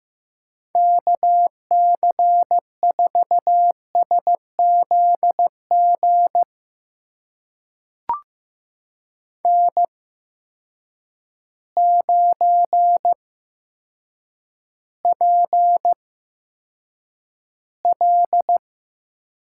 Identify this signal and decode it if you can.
Morse code, heard as a keyed tone, about 15 wpm, spaced as for 8 wpm, 700 Hz. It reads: KC4SZG N9PL
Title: USA Call Signs - 8x Character Spacing - 15wpm